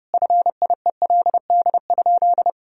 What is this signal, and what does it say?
Morse code, heard as a keyed tone, about 30 wpm, 700 Hz. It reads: FIELD?
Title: Sentences from Top 1000 Words - Review Part 2 - 30wpm